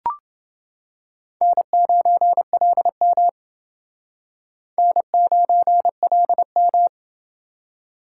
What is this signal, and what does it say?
Morse code, heard as a keyed tone, about 28 wpm, 700 Hz. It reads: N9LM N9LM